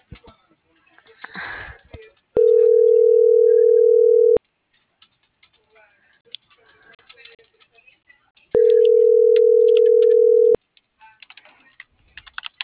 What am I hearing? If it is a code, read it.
Ringback tone (North America), 440 + 480 Hz.